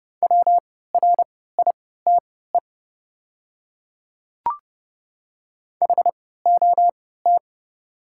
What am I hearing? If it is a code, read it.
Morse code, heard as a keyed tone, about 31 wpm, spaced as for 17 wpm, 700 Hz. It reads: WRITE HOT
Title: 3x Character Spacing - Top 100 Words - 30wpm